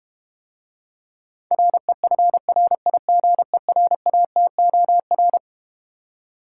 Morse code, heard as a keyed tone, about 33 wpm, 700 Hz. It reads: REFRIGERATOR